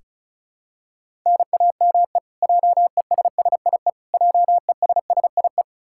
Morse code, heard as a keyed tone, about 36 wpm, 700 Hz. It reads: NAME JESSIE JESSIE
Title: Practice QSO - Exchange 1 - Beginning - 35wpm